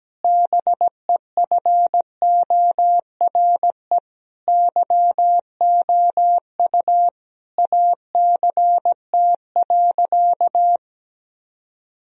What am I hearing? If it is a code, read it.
Morse code, heard as a keyed tone, about 17 wpm, 700 Hz. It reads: BEFORE YOU ACT.
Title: Sentences from Top 1000 Words - Review Part 1 - 17wpm